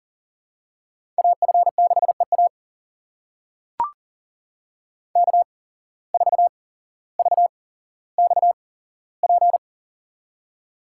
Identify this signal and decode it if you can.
Morse code, heard as a keyed tone, about 40 wpm, spaced as for 17 wpm, 700 Hz. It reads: AF6EA K4VXP
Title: USA Call Signs - 5 Characters - 8x Character Spacing - 40wpm